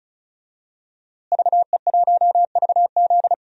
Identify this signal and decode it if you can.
Morse code, heard as a keyed tone, about 34 wpm, 700 Hz. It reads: VE1VZ
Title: DXCC-100 Call Signs - 5 Characters - Repeated 3x - 35wpm